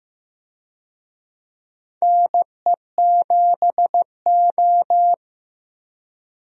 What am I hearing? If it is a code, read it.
Morse code, heard as a keyed tone, about 15 wpm, 700 Hz. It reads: NE7O
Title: USA Call Signs - 4 Characters - Repeated 2x - 15wpm